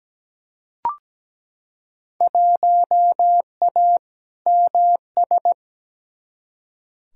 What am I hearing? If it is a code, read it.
Morse code, heard as a keyed tone, about 17 wpm, 700 Hz. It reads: 1A MS
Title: Field Day Exchanges - 17wpm